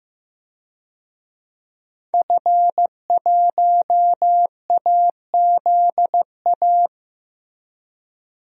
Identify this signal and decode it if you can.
Morse code, heard as a keyed tone, about 15 wpm, 700 Hz. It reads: F1AZA
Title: DXCC-100 Call Signs - 5 Characters - 15wpm